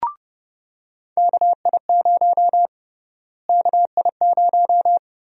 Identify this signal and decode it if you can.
Morse code, heard as a keyed tone, about 30 wpm, 700 Hz. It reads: KI0 KI0
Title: USA Call Sign Prefixes - Repeated 2x - 30wpm